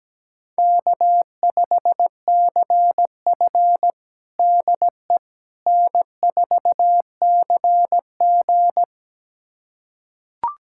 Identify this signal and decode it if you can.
Morse code, heard as a keyed tone, about 17 wpm, 700 Hz. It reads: K5CF DE N4CG
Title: USA Callsign DE Callsign - 4 Characters - 17wpm